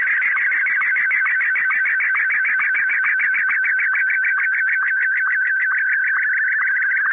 SSTV (slow-scan television), Martin 1.
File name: Z PI4VRZ / PI4VRZ 14-jun-2025